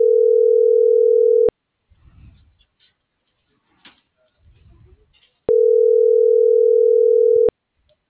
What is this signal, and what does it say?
Ringback tone (North America), 440 + 480 Hz.
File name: CENTRAL MULTIMODAL / out viridiana.juarez - .84263